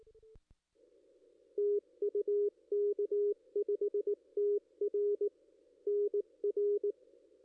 Morse code, heard as a keyed tone, about 18 wpm, 440 Hz. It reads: TU K5TR NR
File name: contests / ss-cw-k5tr-op-n5rz -2300